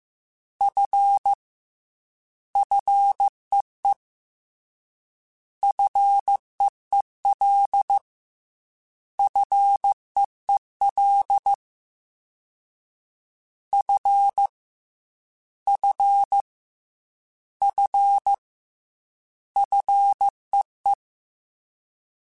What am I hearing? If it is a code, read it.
Morse code, heard as a keyed tone, about 15 wpm, 800 Hz. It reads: F FEE FEEL FEEL F F F FEE
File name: Fs / Single F mp3s / Single F Prefixes 3x2 Twice @13